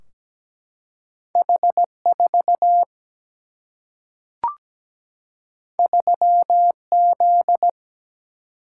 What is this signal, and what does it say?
Morse code, heard as a keyed tone, about 17 wpm, 700 Hz. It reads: H4 3Z
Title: Single-Letter Single-Number 17wpm